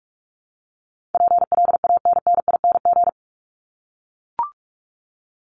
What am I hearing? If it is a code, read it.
Morse code, heard as a keyed tone, about 44 wpm, 700 Hz. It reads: PLANNING
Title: ING Suffix - Top 100 Words with Base Words 45wpm